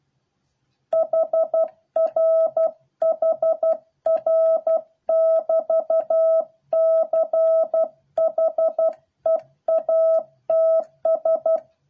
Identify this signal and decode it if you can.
Morse code, heard as a keyed tone, about 12 wpm, 650 Hz. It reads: HR HR <BT> CHEATS